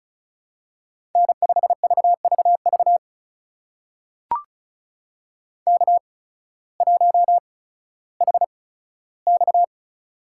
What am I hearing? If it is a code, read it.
Morse code, heard as a keyed tone, about 36 wpm, spaced as for 16 wpm, 700 Hz. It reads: N5VVV K1HX
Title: USA Call Signs - 8x Character Spacing - 35wpm